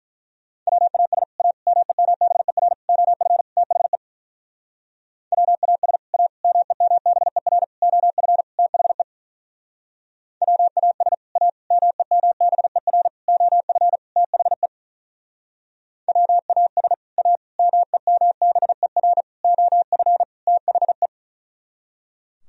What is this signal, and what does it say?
Morse code, heard as a keyed tone, about 43 wpm, 700 Hz. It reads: WAS A MEMBER OF THE WAS A MEMBER OF THE WAS A MEMBER OF THE WAS A MEMBER OF THE
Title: Speed-Racing - Sets of 5 Words 35wpm